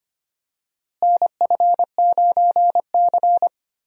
Morse code, heard as a keyed tone, about 25 wpm, 700 Hz. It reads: NF9C
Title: USA Call Signs - 4 Characters - Repeated 3x - 25wpm